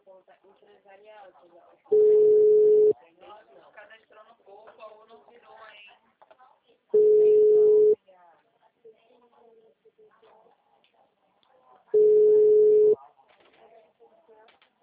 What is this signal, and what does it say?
Ringback tone (Europe, 425 Hz), with speech.
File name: out- -2008- - - .38181